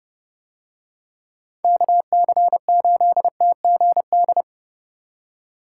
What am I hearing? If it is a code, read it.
Morse code, heard as a keyed tone, about 30 wpm, 700 Hz. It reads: KC8TGD